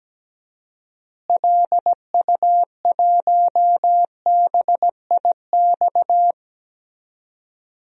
Morse code, heard as a keyed tone, about 17 wpm, 700 Hz. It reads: LU1BIX
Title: DXCC-100 Call Signs - 6 Characters - Repeated 3x - 17wpm